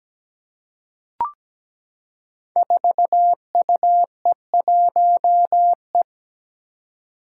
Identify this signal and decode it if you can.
Morse code, heard as a keyed tone, about 17 wpm, 700 Hz. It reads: 4UE1E
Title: Five Challenging Letter-Number Combinations 17wpm